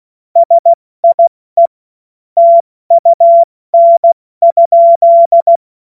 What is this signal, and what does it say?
Morse code, heard as a keyed tone, about 16 wpm, 680 Hz. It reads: SIE TUN?
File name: Morsekurs / Witze / Witz 19 März 2024 70 F80